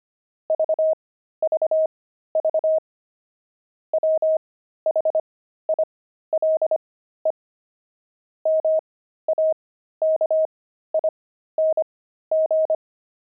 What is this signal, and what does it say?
Morse code, heard as a keyed tone, about 25 wpm, spaced as for 13 wpm, 630 Hz. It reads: VVV WHILE MAKING
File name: pangram-fs-13-cs-25-deep